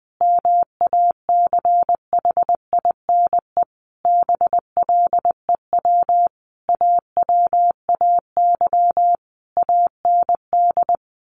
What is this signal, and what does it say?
Morse code, heard as a keyed tone, about 20 wpm, 700 Hz. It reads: MACHINE BLEW AWAY AND
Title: A Sound Of Thunder 20wpm